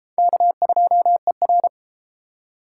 Morse code, heard as a keyed tone, about 30 wpm, 700 Hz. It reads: K2ER